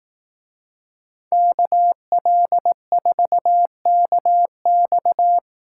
Morse code, heard as a keyed tone, about 18 wpm, 700 Hz. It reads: KL4KX
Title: Speed Racing - USA Call Signs 15wpm